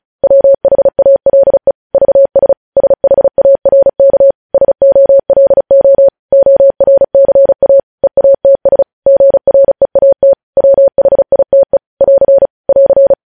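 Morse code, heard as a keyed tone, about 36 wpm, 550 Hz. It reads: WHALE VS SHARK SOLO ORCA EATS GREAT WHITE <AR> <AR>
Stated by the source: Morse code news headlines sent at 35.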